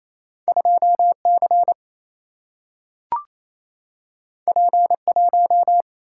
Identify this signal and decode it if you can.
Morse code, heard as a keyed tone, about 28 wpm, 700 Hz. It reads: 2 C P 1
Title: Single-Letter Single-Number 28wpm